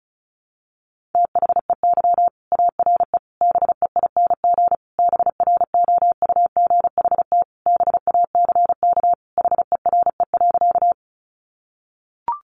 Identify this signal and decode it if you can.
Morse code, heard as a keyed tone, about 34 wpm, 700 Hz. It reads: THEY ARE BEING BROUGHT BACK HERE.
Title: Sentences from Top 600 Words 35wpm